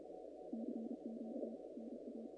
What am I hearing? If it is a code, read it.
Morse code, heard as a keyed tone, about 32 wpm, 230 Hz. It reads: CQ NA
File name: contests / naqp-cw-k5tr-op-k5pi- -2100